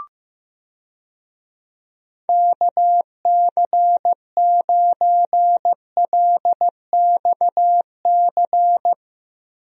Morse code, heard as a keyed tone, about 15 wpm, 700 Hz. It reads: KC9LXC
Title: USA Call Signs - 6 Characters - Repeated 2x - 15wpm